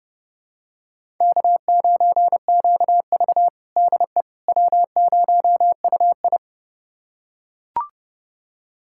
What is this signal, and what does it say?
Morse code, heard as a keyed tone, about 30 wpm, 700 Hz. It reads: K9QV DE W0UI
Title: USA Callsign DE Callsign - 4 Characters - 30wpm